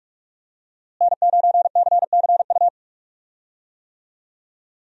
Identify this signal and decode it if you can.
Morse code, heard as a keyed tone, about 44 wpm, 700 Hz. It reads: N9CCU
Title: USA Call Signs - 5 Characters - Repeated 3x - 45wpm